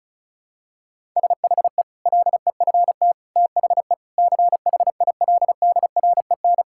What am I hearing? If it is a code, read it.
Morse code, heard as a keyed tone, about 36 wpm, 700 Hz. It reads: SHE LEFT THE CHILDREN
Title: Sentences from Top 1000 Words - Review Part 5 - 35wpm